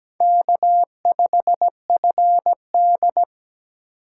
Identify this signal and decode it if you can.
Morse code, heard as a keyed tone, about 20 wpm, 700 Hz. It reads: K5FD
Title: Speed Racing - USA 1x2 Call Signs 17wpm